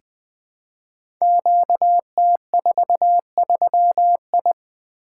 Morse code, heard as a keyed tone, about 21 wpm, 700 Hz. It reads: QT43I
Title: Code Groups - 5 Letter:Numbers - 20wpm